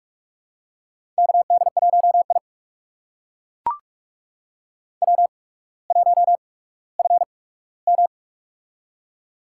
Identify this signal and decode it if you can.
Morse code, heard as a keyed tone, about 41 wpm, spaced as for 19 wpm, 700 Hz. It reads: KD1I W1FM
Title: USA Call Signs - 4 Characters - 8x Character Spacing - 45wpm